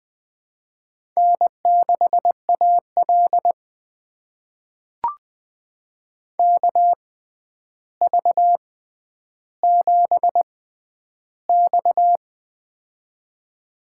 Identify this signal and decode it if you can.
Morse code, heard as a keyed tone, about 20 wpm, spaced as for 10 wpm, 700 Hz. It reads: N6AL KV7X
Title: USA Call Signs - 4 Characters - 6x Character Spacing - 20wpm